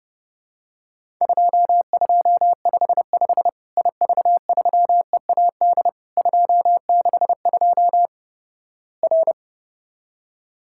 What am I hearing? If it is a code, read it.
Morse code, heard as a keyed tone, about 32 wpm, 700 Hz. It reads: 2255 IV3EAD 262 R
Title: Worked All Europe DX Contest - QTC Exchange - 30wpm